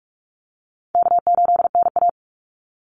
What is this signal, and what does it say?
Morse code, heard as a keyed tone, about 44 wpm, 700 Hz. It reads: K8NA
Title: Easy Call Signs Repeated 2x - 45wpm